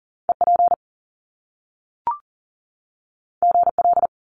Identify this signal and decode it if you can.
Morse code, heard as a keyed tone, about 42 wpm, 700 Hz. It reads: EP GL
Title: Top Two Letter Combinations in Words 40wpm